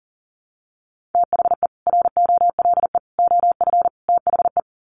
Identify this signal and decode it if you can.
Morse code, heard as a keyed tone, about 40 wpm, 700 Hz. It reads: THE ROLE OF THE
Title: Sets of 4 Words 40wpm